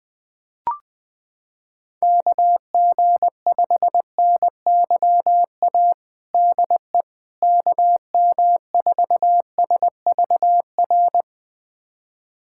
Morse code, heard as a keyed tone, about 20 wpm, 700 Hz. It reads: KG5NYA DE KM4SVR
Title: USA Callsign DE Callsign - 6 Characters - 20wpm